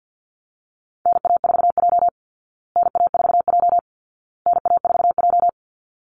Morse code, heard as a keyed tone, about 50 wpm, 700 Hz. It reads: NA4J NA4J NA4J
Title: Call Signs Repeated 2x - 50wpm